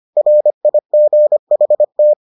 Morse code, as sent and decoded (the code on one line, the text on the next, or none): .-. .. --. .... -
RIGHT